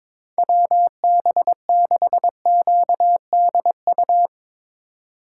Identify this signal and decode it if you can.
Morse code, heard as a keyed tone, about 22 wpm, 700 Hz. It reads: WB6QDU
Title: Speed Racing - USA 2x3 Call Signs 22wpm